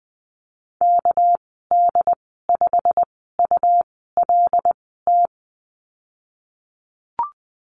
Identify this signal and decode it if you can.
Morse code, heard as a keyed tone, about 19 wpm, spaced as for 14 wpm, 700 Hz. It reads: KD5ULT
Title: 213 Lesson 34g - Number 8 - Call Signs 20wpm